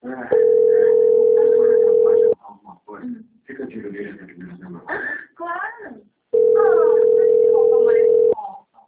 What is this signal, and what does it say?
Ringback tone (North America), 440 + 480 Hz, with speech.
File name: rg-1600-unknown- - - .16427